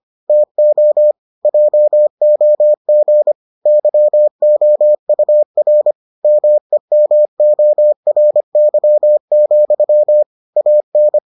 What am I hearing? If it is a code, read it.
Morse code, heard as a keyed tone, about 25 wpm, 600 Hz. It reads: TO JOG YOUR MEMORY, AN